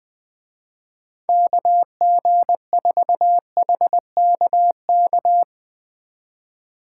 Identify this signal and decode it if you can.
Morse code, heard as a keyed tone, about 22 wpm, 700 Hz. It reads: KG4HKK
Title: Speed Racing - USA 2x3 Call Signs 20wpm